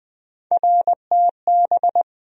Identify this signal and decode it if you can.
Morse code, heard as a keyed tone, about 21 wpm, 700 Hz. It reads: RTB
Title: USA Call Sign Postfixes - 3 Letters 20wpm